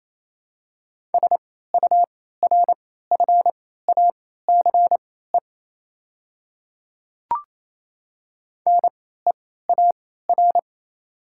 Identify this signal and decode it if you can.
Morse code, heard as a keyed tone, about 28 wpm, spaced as for 16 wpm, 700 Hz. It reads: SURFACE NEAR
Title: 3x Character Spacing - Top 500 Words - 28wpm